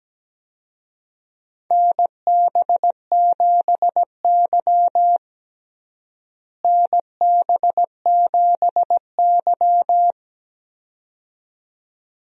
Morse code, heard as a keyed tone, about 17 wpm, 700 Hz. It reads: NB7Y NB7Y
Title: USA Call Signs - 4 Characters - Repeated 2x - 17wpm